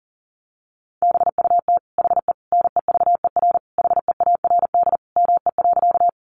Morse code, heard as a keyed tone, about 40 wpm, 700 Hz. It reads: BUT HE NEVER HEARD ME.
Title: Sentences from Top 400 Words 40wpm